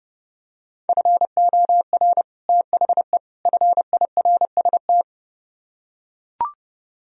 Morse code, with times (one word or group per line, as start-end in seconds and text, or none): 0.9-2.2 FOR
2.5-3.2 THE
3.5-5.0 FIRST